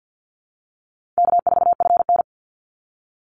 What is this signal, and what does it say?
Morse code, heard as a keyed tone, about 50 wpm, 700 Hz. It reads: K4FN
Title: Call Signs - 50wpm